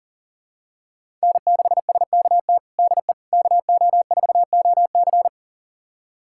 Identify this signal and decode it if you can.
Morse code, heard as a keyed tone, about 40 wpm, 700 Hz. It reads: N6SKT DE KO4OC
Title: USA Callsign DE Callsign - 5 Characters - 40wpm